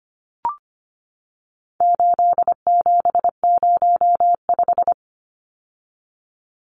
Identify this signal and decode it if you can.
Morse code, heard as a keyed tone, about 25 wpm, 700 Hz. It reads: 8705
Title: Numbers - 4-Digit 25wpm